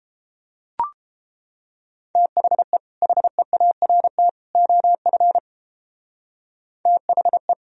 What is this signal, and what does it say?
Morse code, heard as a keyed tone, about 32 wpm, 700 Hz. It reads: THE HEART OF THE